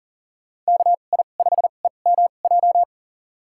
Morse code, heard as a keyed tone, about 40 wpm, spaced as for 29 wpm, 700 Hz. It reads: KI5EMJ
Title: USA Call Signs - 6 Characters - 2x Character Spacing - 40wpm